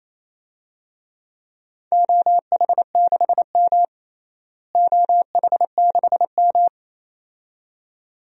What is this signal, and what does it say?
Morse code, heard as a keyed tone, about 28 wpm, 700 Hz. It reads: OH6M OH6M
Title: DXCC-100 Call Signs - 4 Characters - Repeated 2x - 28wpm